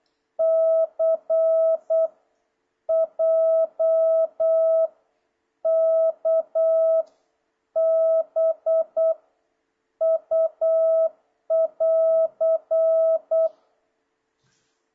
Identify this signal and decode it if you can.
Morse code, heard as a keyed tone, about 8 wpm, 650 Hz. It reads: C J K B U <AR>
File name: jh8cbh / 2026cw